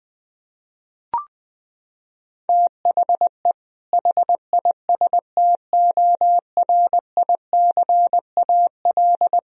Morse code, heard as a keyed tone, about 21 wpm, 700 Hz. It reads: THE HISTORICAL
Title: Sentences from Top 1000 Words - Review Part 4 - 20wpm